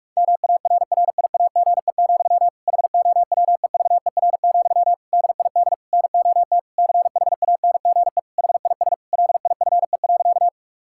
Morse code, heard as a keyed tone, about 46 wpm, 700 Hz. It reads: MARRIAGE, HOWEVER, DID NOT CHANGE HIS LIFE.
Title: Sentences from Top 1000 Words - Review Part 5 - 45wpm